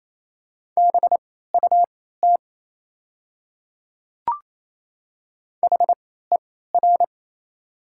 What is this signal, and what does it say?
Morse code, heard as a keyed tone, about 28 wpm, spaced as for 16 wpm, 700 Hz. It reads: BUT HER
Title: 3x Character Spacing - Top 100 Words - 28wpm